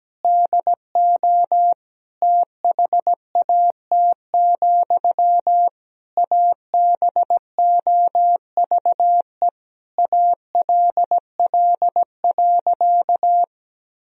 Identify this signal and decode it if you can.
Morse code, heard as a keyed tone, about 17 wpm, 700 Hz. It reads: DO THAT, ABOVE ALL.
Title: Sentences from Top 1000 Words - Review Part 2 - 17wpm